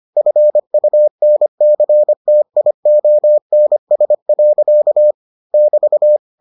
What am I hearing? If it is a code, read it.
Morse code, heard as a keyed tone, about 25 wpm, 600 Hz. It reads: FUNCTIONS. <BT>